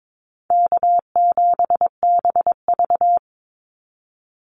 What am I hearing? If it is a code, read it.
Morse code, heard as a keyed tone, about 22 wpm, 700 Hz. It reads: K7BV
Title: Easy Call Signs Repeated 3x - 22wpm